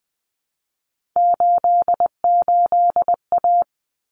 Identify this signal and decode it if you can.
Morse code, heard as a keyed tone, about 20 wpm, 700 Hz. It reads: 88A
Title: Common Cut Numbers - 3-Digit 20wpm